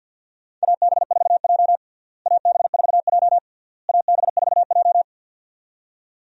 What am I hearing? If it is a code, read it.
Morse code, heard as a keyed tone, about 52 wpm, 700 Hz. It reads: AB4J AB4J AB4J
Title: USA Call Signs - 4 Characters - Repeated 3x - 50wpm